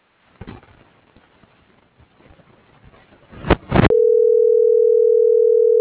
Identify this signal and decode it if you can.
Ringback tone (North America), 440 + 480 Hz.